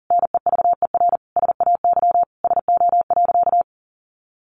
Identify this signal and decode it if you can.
Morse code, heard as a keyed tone, about 40 wpm, 700 Hz. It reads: NEVER SAY SO.
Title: Sentences from Top 1400 Words 40wpm